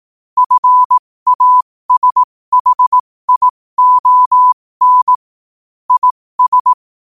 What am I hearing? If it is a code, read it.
Morse code, heard as a keyed tone, about 18 wpm, spaced as for 15 wpm, 1000 Hz. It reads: FASHION IS
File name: qotdfiles / 15wpm